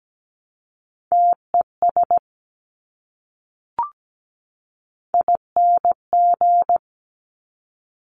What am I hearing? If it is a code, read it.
Morse code, heard as a keyed tone, about 17 wpm, 700 Hz. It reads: TES ING
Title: Top Three Letter Combinations in Words 17wpm